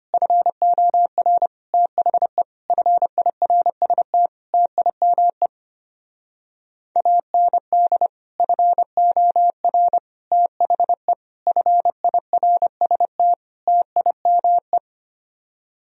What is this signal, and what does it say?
Morse code, heard as a keyed tone, about 27 wpm, 700 Hz. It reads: FOR THE FIRST TIME AND FOR THE FIRST TIME
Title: Speed-Racing - Sets of 5 Words 25wpm